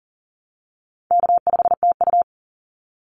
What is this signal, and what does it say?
Morse code, heard as a keyed tone, about 40 wpm, 700 Hz. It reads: K5TU
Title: Easy Call Signs Repeated 2x - 40wpm